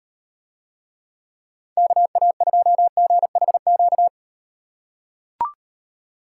Morse code, heard as a keyed tone, about 38 wpm, 700 Hz. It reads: KA2GHQ